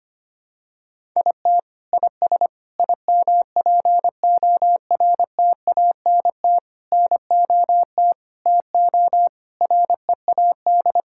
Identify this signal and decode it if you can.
Morse code, heard as a keyed tone, about 25 wpm, 700 Hz. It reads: IT IS IMPORTANT NOT TO READ
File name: Sets of 9 Words - 25wpm